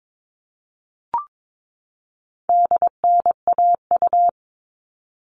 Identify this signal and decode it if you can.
Morse code, heard as a keyed tone, about 22 wpm, 700 Hz. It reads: DNAU
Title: All Cut Numbers - 4-Digit 22wpm